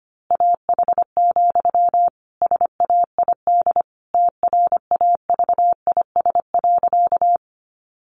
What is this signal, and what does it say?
Morse code, heard as a keyed tone, about 25 wpm, 700 Hz. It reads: AH, SAID TRAVIS.
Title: A Sound Of Thunder 25wpm